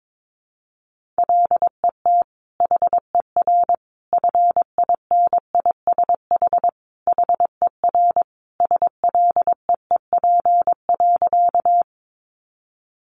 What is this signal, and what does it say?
Morse code, heard as a keyed tone, about 22 wpm, 700 Hz. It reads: LET HER FINISH HER SLEEP.
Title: Sentences from Top 1200 Words 22wpm